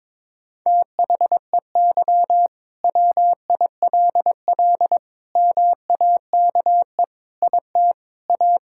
Morse code, heard as a keyed tone, about 22 wpm, 700 Hz. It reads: THEY WILL MAKE IT A
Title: Sentences from Top 1600 Words 22wpm